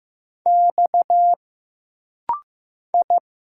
Morse code, heard as a keyed tone, about 15 wpm, 700 Hz. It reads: X I
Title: Single Letters - Rapid-Fire 15wpm